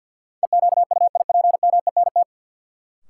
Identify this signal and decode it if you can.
Morse code, heard as a keyed tone, about 48 wpm, 700 Hz. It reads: EQUIPMENT
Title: Sentences from Top 700 Words - New Word Drill - 50wpm